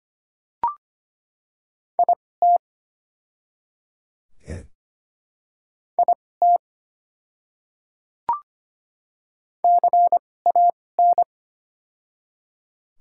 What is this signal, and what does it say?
Morse code, heard as a keyed tone, about 25 wpm, spaced as for 18 wpm, 700 Hz. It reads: IT IT CAN
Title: 2x Character Spacing - Top 100 Words - 25wpm